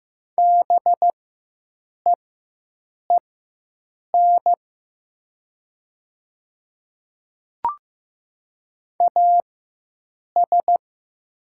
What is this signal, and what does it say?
Morse code, heard as a keyed tone, about 15 wpm, spaced as for 7 wpm, 700 Hz. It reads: BEEN AS
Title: 4x Character Spacing - Top 100 Words - 15wpm